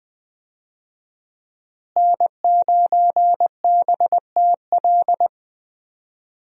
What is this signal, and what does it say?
Morse code, heard as a keyed tone, about 21 wpm, 700 Hz. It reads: N9BTL
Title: USA Call Signs - 5 Characters - Repeated 3x - 20wpm